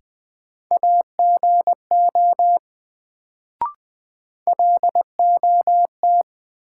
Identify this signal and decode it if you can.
Morse code, heard as a keyed tone, about 20 wpm, 700 Hz. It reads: AGO LOT
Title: Sentences from Top 500 Words - New Word Drill - 20wpm